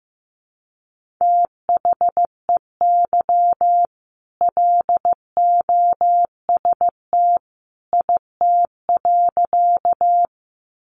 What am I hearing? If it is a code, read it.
Morse code, heard as a keyed tone, about 14 wpm, 700 Hz. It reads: THEY LOST IT.
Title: Sentences from Top 1400 Words 15wpm